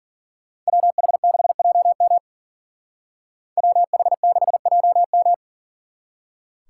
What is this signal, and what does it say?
Morse code, heard as a keyed tone, about 49 wpm, 700 Hz. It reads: WH6JM WH6JM
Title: Speed Racing - USA Call Signs 40wpm